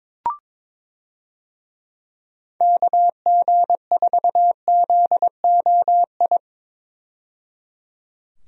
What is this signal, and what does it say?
Morse code, heard as a keyed tone, about 21 wpm, 700 Hz. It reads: KG4ZOI
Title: USA Call Signs - 6 Characters - 22wpm